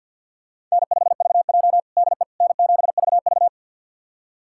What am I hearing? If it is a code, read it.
Morse code, heard as a keyed tone, about 53 wpm, 700 Hz. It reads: N5VJ DE N7VV
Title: USA Callsign DE Callsign - 4 Characters - 50wpm